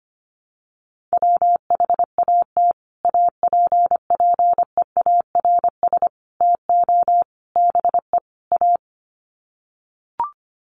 Morse code, heard as a keyed tone, about 25 wpm, 700 Hz. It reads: WHAT APPEARS TO BE A
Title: Sets of 5 Words Encore2 25wpm